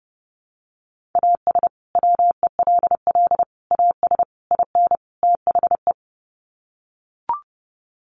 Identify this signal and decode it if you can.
Morse code, heard as a keyed tone, about 30 wpm, 700 Hz. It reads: AS WELL AS IN THE
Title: Sets of 5 Words Encore 30wpm